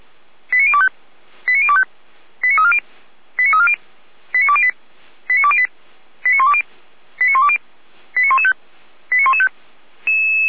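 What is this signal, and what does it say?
Five-tone selective call (ZVEI), heard as quick runs of five tones, tones of about 70 ms. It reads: 89025 89025 89340 89340 89208 89208 89120 89120 89115 89115
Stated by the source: ZVEI five-tone call 89092 received by SDR.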